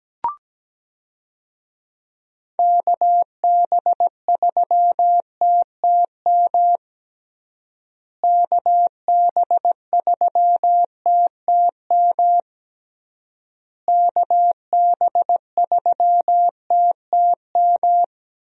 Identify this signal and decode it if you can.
Morse code, heard as a keyed tone, about 17 wpm, 700 Hz. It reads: KB3TTM KB3TTM KB3TTM
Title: USA Call Signs - 6 Characters - Repeated 3x - 17wpm